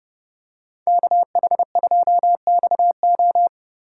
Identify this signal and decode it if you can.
Morse code, heard as a keyed tone, about 31 wpm, 700 Hz. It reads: KH2XO